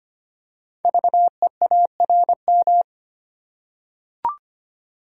Morse code, heard as a keyed tone, about 25 wpm, 700 Hz. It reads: VEARM
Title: Code Groups - 5 Letters - 25wpm